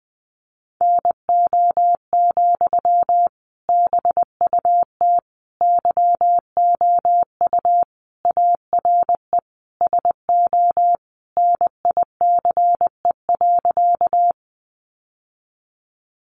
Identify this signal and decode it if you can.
Morse code, heard as a keyed tone, about 20 wpm, 700 Hz. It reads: NO, BUT YOU ARE SO NICE.
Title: Sentences from Top 1100 Words 20wpm